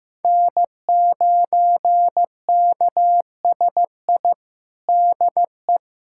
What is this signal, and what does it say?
Morse code, heard as a keyed tone, about 15 wpm, 700 Hz. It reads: N9KSI DE
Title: USA Callsign DE Callsign - 5 Characters - 15wpm